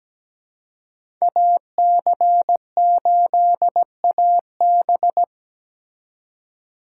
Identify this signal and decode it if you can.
Morse code, heard as a keyed tone, about 20 wpm, 700 Hz. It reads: AC8AB
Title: Speed Racing - USA Call Signs 17wpm